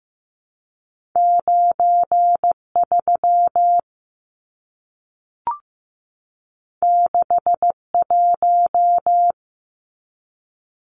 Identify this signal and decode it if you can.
Morse code, heard as a keyed tone, about 15 wpm, 700 Hz. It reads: 9 3 6 1
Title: Numbers - 2-Digit 15wpm